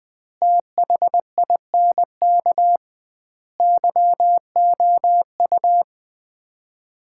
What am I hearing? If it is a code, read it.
Morse code, heard as a keyed tone, about 20 wpm, 700 Hz. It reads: THINK YOU
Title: 2x Word-Spacing - Sets of 4 Words - 20wpm